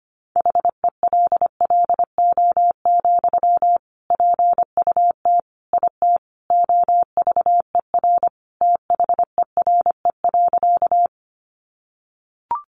Morse code, heard as a keyed tone, about 25 wpm, 700 Hz. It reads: HELLO, PUT IT OVER THERE.
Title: Sentences from Top 1100 Words 25wpm